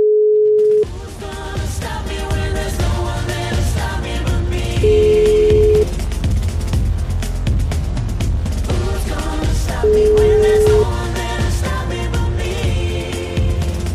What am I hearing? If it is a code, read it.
Ringback tone (Europe, 425 Hz), with noise.